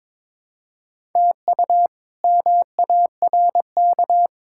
Morse code, heard as a keyed tone, about 22 wpm, 700 Hz. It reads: TU MARK
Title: K1USN SST QSO Practice - 22wpm